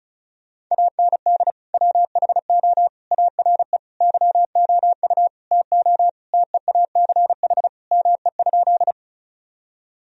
Morse code, heard as a keyed tone, about 36 wpm, 700 Hz. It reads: AND WHO ARE YOU TO TEACH ME?
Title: Sentences from Top 1000 Words - Review Part 2 - 35wpm